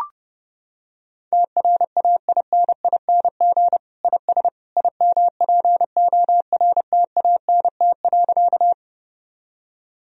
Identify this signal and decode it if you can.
Morse code, heard as a keyed tone, about 29 wpm, 700 Hz. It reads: TRAINING IS IMPORTANT.
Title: Sentences from Top 1000 Words - Review Part 4 - 30wpm